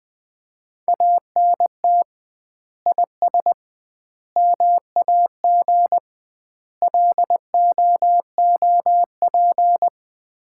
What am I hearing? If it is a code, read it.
Morse code, heard as a keyed tone, about 20 wpm, 700 Hz. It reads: ANT IS MAG LOOP
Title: Common QSO Phrases - 2x Word-Spacing - 20wpm